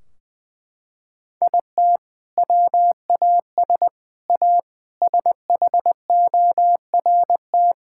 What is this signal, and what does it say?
Morse code, heard as a keyed tone, about 20 wpm, 700 Hz. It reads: IT WAS A SHORT
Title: Sentences from Top 1600 Words 20wpm